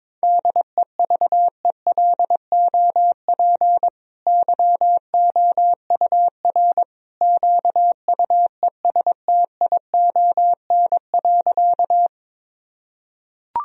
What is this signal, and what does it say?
Morse code, heard as a keyed tone, about 22 wpm, 700 Hz. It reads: DEVELOP YOUR QUESTION.
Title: Sentences from Top 1000 Words - Review Part 2 - 22wpm